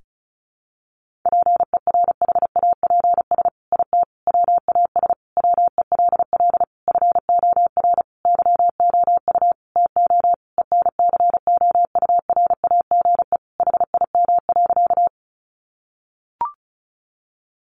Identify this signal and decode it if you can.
Morse code, heard as a keyed tone, about 34 wpm, 700 Hz. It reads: PERHAPS IT WAS WELL FOR YOU TO ENCOURAGE HIM.
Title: Sentences from Top 1400 Words 35wpm